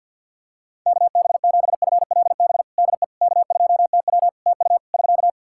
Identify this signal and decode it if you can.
Morse code, heard as a keyed tone, about 47 wpm, 700 Hz. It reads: KB7FLB DE K2TW TU <SK>
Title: Practice QSO - Exchange 4 - Ending - 50wpm